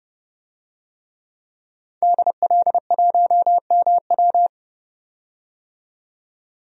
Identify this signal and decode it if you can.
Morse code, heard as a keyed tone, about 32 wpm, 700 Hz. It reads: DL1MW
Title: DXCC-100 Call Signs - 5 Characters - 30wpm